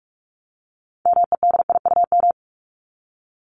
Morse code, heard as a keyed tone, about 44 wpm, 700 Hz. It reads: MEDIUM
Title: Top 751-1000 Words 45wpm